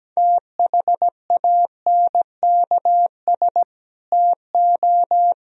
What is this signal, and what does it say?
Morse code, heard as a keyed tone, about 17 wpm, 700 Hz. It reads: THANKS TO
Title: Sentences from Top 1000 Words - Review Part 3 - 17wpm